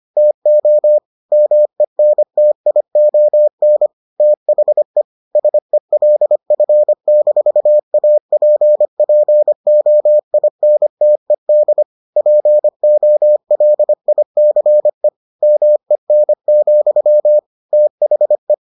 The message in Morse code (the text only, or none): TO MENTION THE SELF-APPOINTED POLICE MEN, THE